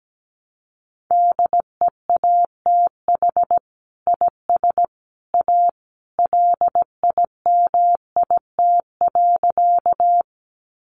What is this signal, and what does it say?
Morse code, heard as a keyed tone, about 17 wpm, 700 Hz. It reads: DEATH IS A LIMIT.
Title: Sentences from Top 1500 Words 17wpm